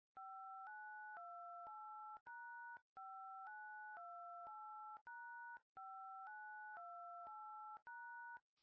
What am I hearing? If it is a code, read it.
DTMF, heard as telephone touch tones, tones of about 510 ms. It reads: 5928#5928#5928#